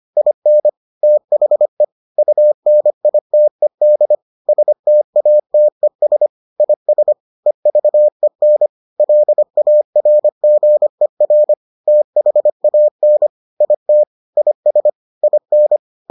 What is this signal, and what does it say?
Morse code, heard as a keyed tone, about 25 wpm, 600 Hz. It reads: IN THE UNITED STATES IS EVEN LARGER THAN IT IS IN